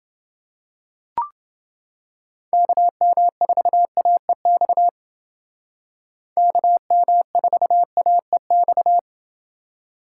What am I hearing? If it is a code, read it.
Morse code, heard as a keyed tone, about 29 wpm, 700 Hz. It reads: KM4AEX KM4AEX